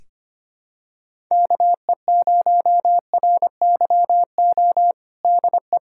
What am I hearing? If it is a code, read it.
Morse code, heard as a keyed tone, about 25 wpm, 700 Hz. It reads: KE0RYO DE
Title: USA Callsign DE Callsign 25wpm